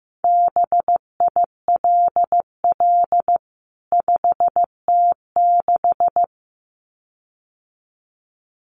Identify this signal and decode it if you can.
Morse code, heard as a keyed tone, about 15 wpm, 700 Hz. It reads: BILL 5T6
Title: CWT Exchange 15wpm